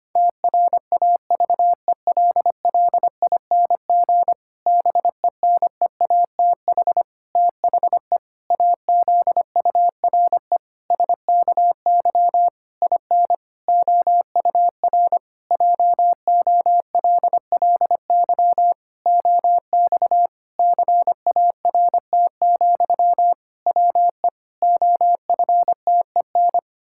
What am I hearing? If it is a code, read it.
Morse code, heard as a keyed tone, about 25 wpm, 700 Hz. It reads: TRAVELLING BENEATH THE AZURE SKY IN OUR JOLLY OX CART, WE OFTEN
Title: Pangram Challenge v2 - 25wpm